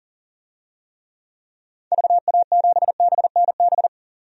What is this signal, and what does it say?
Morse code, heard as a keyed tone, about 40 wpm, 700 Hz. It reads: VA7BNB
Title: DXCC-100 Call Signs - 6 Characters - 40wpm